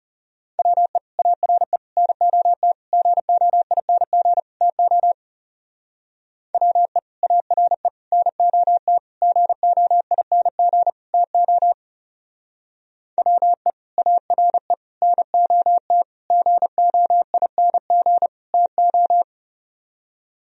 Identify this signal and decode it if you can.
Morse code, heard as a keyed tone, about 34 wpm, 700 Hz. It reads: WE ARE NOT GOING TO WE ARE NOT GOING TO WE ARE NOT GOING TO
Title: Speed-Racing - Sets of 5 Words 30wpm